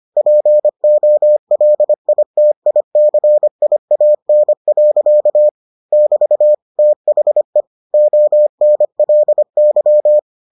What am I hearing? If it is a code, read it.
Morse code, heard as a keyed tone, about 25 wpm, 600 Hz. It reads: POLITICIAN. <BT> THE ONLY